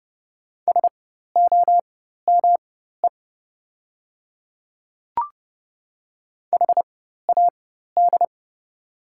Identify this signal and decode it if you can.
Morse code, heard as a keyed tone, about 30 wpm, spaced as for 14 wpm, 700 Hz. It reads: SOME HAD
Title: 4x Character Spacing - Top 100 Words - 30wpm